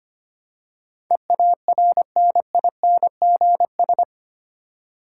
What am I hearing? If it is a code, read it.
Morse code, heard as a keyed tone, about 25 wpm, 700 Hz. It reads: EARNINGS
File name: Eight Letter Words - 25wpm